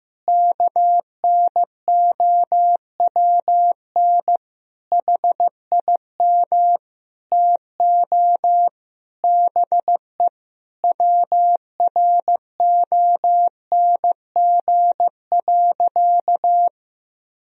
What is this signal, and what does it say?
Morse code, heard as a keyed tone, about 15 wpm, 700 Hz. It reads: KNOWN HIM TO BE WRONG.
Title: Sentences from Top 1000 Words - Review Part 3 - 15wpm